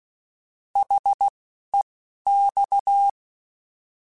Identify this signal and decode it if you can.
Morse code, heard as a keyed tone, about 16 wpm, spaced as for 11 wpm, 800 Hz. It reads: HEX
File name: Alphabetical / Alphabetical mp3s / Hs @10